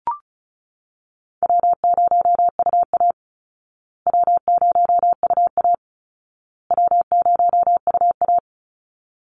Morse code, heard as a keyed tone, about 34 wpm, 700 Hz. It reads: W0UA W0UA W0UA
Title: Easy Call Signs Repeated 3x - 35wpm